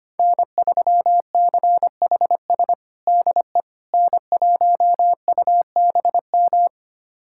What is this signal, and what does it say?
Morse code, heard as a keyed tone, about 25 wpm, 700 Hz. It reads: N3CHS DE N1UBM
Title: USA Callsign DE Callsign - 5 Characters - 25wpm